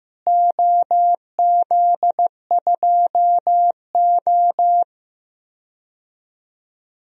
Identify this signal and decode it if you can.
Morse code, heard as a keyed tone, about 15 wpm, 700 Hz. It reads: OZ2O
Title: DXCC-100 Call Signs - 4 Characters - Repeated 2x - 15wpm